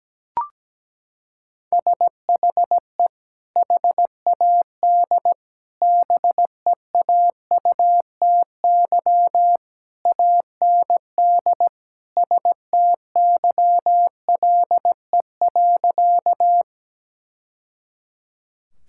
Morse code, heard as a keyed tone, about 17 wpm, 700 Hz. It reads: SHE HAD BEAUTY AND STYLE.
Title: Sentences from Top 1000 Words - Review Part 5 - 17wpm